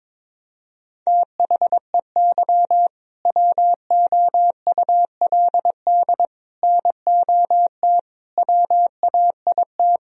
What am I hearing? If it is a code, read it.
Morse code, heard as a keyed tone, about 22 wpm, 700 Hz. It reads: THEY WOULD NOT WAIT
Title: Sentences from Top 1000 Words - Review Part 3 - 22wpm